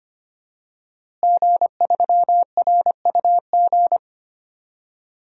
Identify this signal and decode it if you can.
Morse code, heard as a keyed tone, about 25 wpm, 700 Hz. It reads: G3RUG
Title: DXCC-100 Call Signs - 5 Characters - Repeated 2x - 25wpm